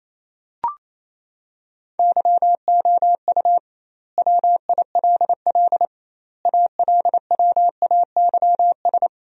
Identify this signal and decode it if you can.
Morse code, heard as a keyed tone, about 28 wpm, 700 Hz. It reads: YOU WILL ALWAYS
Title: 2x Word-Spacing - Sentences from Top 400 Words - 28wpm